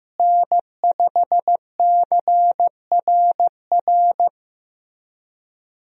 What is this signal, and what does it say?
Morse code, heard as a keyed tone, about 16 wpm, 700 Hz. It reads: N5CRR
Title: USA Call Signs - 5 Characters - Repeated 3x - 15wpm